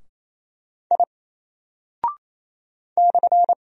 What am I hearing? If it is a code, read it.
Morse code, heard as a keyed tone, about 27 wpm, 700 Hz. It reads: I /
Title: Single Character (Letter:Number:Punctuation:Prosign) 28wpm